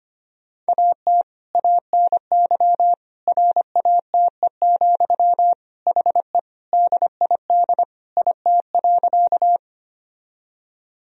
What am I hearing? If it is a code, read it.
Morse code, heard as a keyed tone, about 25 wpm, 700 Hz. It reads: AT ANY RATE, HE DID IT.
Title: Sentences from Top 1000 Words - Review Part 4 - 25wpm